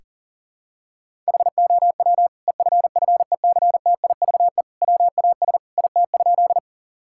Morse code, heard as a keyed tone, about 40 wpm, 700 Hz. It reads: HOW EFFECTIVE WAS IT?
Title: Sentences from Top 1000 Words - Review Part 6 - 40wpm